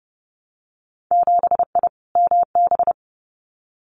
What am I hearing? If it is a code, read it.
Morse code, heard as a keyed tone, about 30 wpm, 700 Hz. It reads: 7I MB
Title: Winter Field Day Exchanges Encore 30wpm